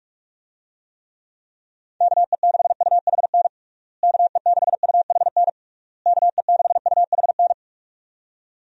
Morse code, heard as a keyed tone, about 46 wpm, 700 Hz. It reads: KE6UHN KE6UHN KE6UHN
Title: USA Call Signs - 6 Characters - Repeated 3x - 45wpm